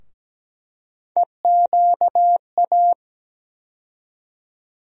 Morse code, heard as a keyed tone, about 17 wpm, 700 Hz. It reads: EQA
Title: USA Call Sign Postfixes - 3 Letters 17wpm